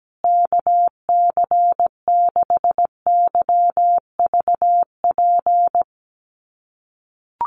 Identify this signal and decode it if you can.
Morse code, heard as a keyed tone, about 17 wpm, 700 Hz. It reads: KC6YVP
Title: USA Call Signs Repeated 3x 17wpm